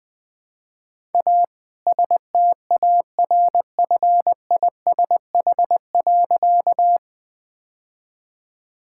Morse code, heard as a keyed tone, about 20 wpm, 700 Hz. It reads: A STARFISH.
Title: All-Age Appropriate Jokes - 20wpm